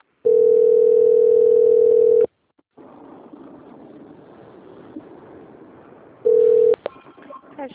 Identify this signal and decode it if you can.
Ringback tone (North America), 440 + 480 Hz.
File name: out- - - - - .26643